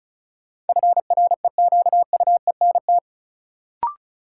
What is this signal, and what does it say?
Morse code, heard as a keyed tone, about 36 wpm, 700 Hz. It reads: FREQUENT
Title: Sentences from Top 1400 Words - New Word Drill Alt - 35wpm